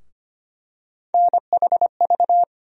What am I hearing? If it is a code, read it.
Morse code, heard as a keyed tone, about 25 wpm, 700 Hz. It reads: NHV
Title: USA Call Sign Postfixes - 3 Letters 25wpm